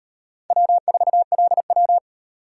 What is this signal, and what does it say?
Morse code, heard as a keyed tone, about 36 wpm, 700 Hz. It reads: W4LW